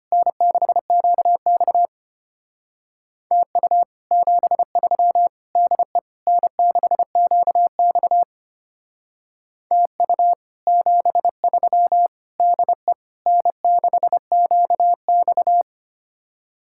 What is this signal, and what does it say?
Morse code, heard as a keyed tone, about 33 wpm, 700 Hz. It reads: N6QX TU 73 DE N6QX TU 73 DE N6QX